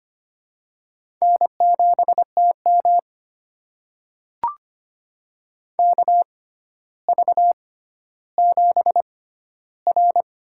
Morse code, heard as a keyed tone, about 25 wpm, spaced as for 13 wpm, 700 Hz. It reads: N7TM KV7R
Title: USA Call Signs - 4 Characters - 6x Character Spacing - 25wpm